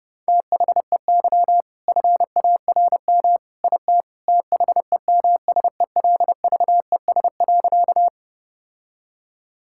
Morse code, heard as a keyed tone, about 30 wpm, 700 Hz. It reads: THEY FARM IT THEMSELVES.
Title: Sentences from Top 1000 Words - Review Part 5 - 30wpm